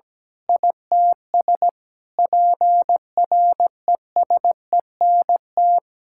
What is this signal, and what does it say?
Morse code, heard as a keyed tone, about 17 wpm, 700 Hz. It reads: ITS PRESENT
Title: Sentences from Top 1000 Words - Review Part 6 - 17wpm